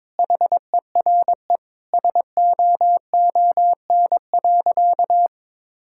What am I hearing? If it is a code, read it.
Morse code, heard as a keyed tone, about 22 wpm, 700 Hz. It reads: HERE SOON.
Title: Sentences from Top 1600 Words 22wpm